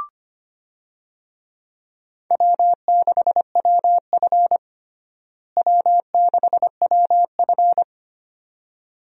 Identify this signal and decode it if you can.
Morse code, heard as a keyed tone, about 25 wpm, 700 Hz. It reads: W6WF W6WF
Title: USA Call Signs - 4 Characters - Repeated 2x - 25wpm